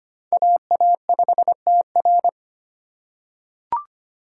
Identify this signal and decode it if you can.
Morse code, heard as a keyed tone, about 25 wpm, 700 Hz. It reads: AA5TR
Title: USA Call Signs - 5 Characters - 6x Character Spacing - 25wpm